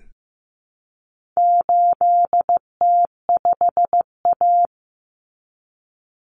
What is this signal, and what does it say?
Morse code, heard as a keyed tone, about 15 wpm, 700 Hz. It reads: 8T5A
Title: Common Cut Numbers - 4-Digit 15wpm